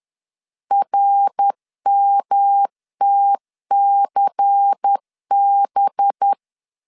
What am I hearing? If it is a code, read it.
Morse code, heard as a keyed tone, about 10 wpm, 800 Hz. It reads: RMTCB
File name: LEZIONI / 6 PKX0 / 6 P 10wpm 2432